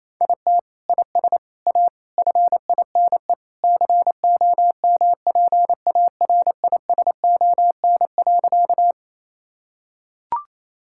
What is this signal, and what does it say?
Morse code, heard as a keyed tone, about 28 wpm, 700 Hz. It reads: IT IS A FINE COMPARISON.
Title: Sentences from Top 1600 Words 28wpm